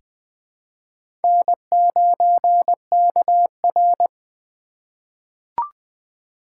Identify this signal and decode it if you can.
Morse code, heard as a keyed tone, about 21 wpm, 700 Hz. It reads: N9KR
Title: K1USN SST QSO Practice - 20wpm